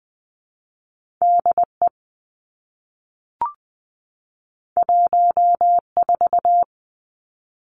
Morse code, heard as a keyed tone, about 20 wpm, 700 Hz. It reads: DE 14
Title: Two Challenging Letter-Number Combinations 20wpm